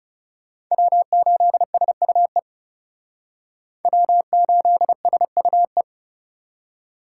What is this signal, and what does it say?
Morse code, heard as a keyed tone, about 36 wpm, 700 Hz. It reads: W8SUE W8SUE
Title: Speed Racing - USA Call Signs 30wpm